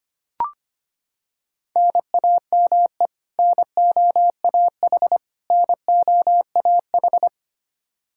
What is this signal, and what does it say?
Morse code, heard as a keyed tone, about 25 wpm, 700 Hz. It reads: NAME NOAH NOAH
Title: Practice QSO - Exchange 1 - Beginning - 25wpm